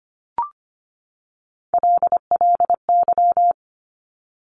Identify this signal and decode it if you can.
Morse code, heard as a keyed tone, about 25 wpm, 700 Hz. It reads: LLY
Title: Top Three Letter Combinations in Words 25wpm